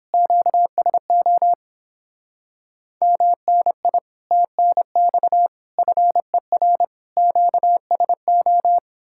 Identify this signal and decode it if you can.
Morse code, heard as a keyed tone, about 31 wpm, 700 Hz. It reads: QSO MNI TNX FER QSO